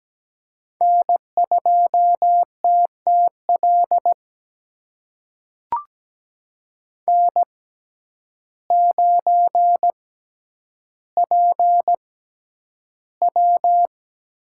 Morse code, heard as a keyed tone, about 17 wpm, spaced as for 9 wpm, 700 Hz. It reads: N2TTL N9PW
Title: USA Call Signs - 5 Characters - 6x Character Spacing - 17wpm